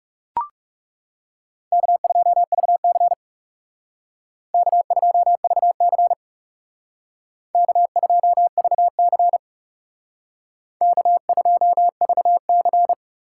Morse code, heard as a keyed tone, about 37 wpm, 700 Hz. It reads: K2VC K2VC K2VC K2VC
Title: Speed Racing - USA 1x2 Call Signs 30wpm